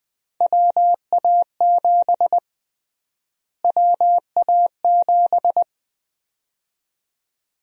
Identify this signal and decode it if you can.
Morse code, heard as a keyed tone, about 21 wpm, 700 Hz. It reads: WA7 WA7
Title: USA Call Sign Prefixes - Repeated 2x - 20wpm